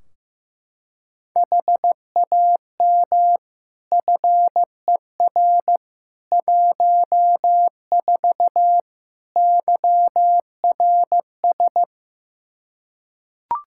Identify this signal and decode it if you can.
Morse code, heard as a keyed tone, about 15 wpm, 700 Hz. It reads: HAM FER 14 YRS
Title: K1USN SST QSO Practice 15wpm